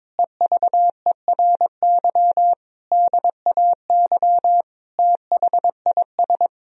Morse code, heard as a keyed tone, about 22 wpm, 700 Hz. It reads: EVERY DAY THIS
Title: Sentences from Top 1000 Words - Review Part 2 - 22wpm